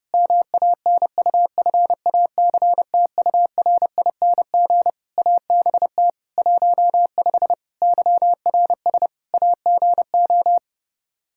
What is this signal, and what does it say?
Morse code, heard as a keyed tone, about 30 wpm, 700 Hz. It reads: MANUFACTURING ABT 15 YRS AGO
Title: Practice QSO - Exchange 3 - More Info - 30wpm